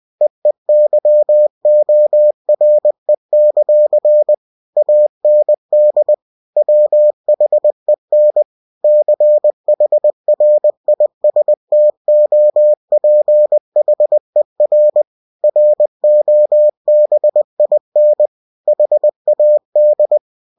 Morse code, heard as a keyed tone, about 20 wpm, 600 Hz. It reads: EEYORE; AND WHEN CHRISTOPHER ROBIN HAD
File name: cw-books / winnie-the-pooh / winnie-20wpm-page-0032